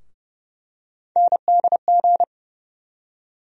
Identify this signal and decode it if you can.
Morse code, heard as a keyed tone, about 30 wpm, 700 Hz. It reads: NDG
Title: USA Call Sign Postfixes - 3 Letters 30wpm